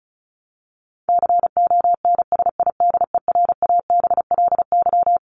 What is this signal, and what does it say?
Morse code, heard as a keyed tone, about 34 wpm, 700 Hz. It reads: CONSIDERABLY
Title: LY Suffix - Top 100 Words with Base Words 35wpm